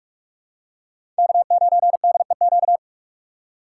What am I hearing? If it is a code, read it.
Morse code, heard as a keyed tone, about 41 wpm, 700 Hz. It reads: K9DEQ
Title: USA Call Signs - 5 Characters - 4x Character Spacing - 45wpm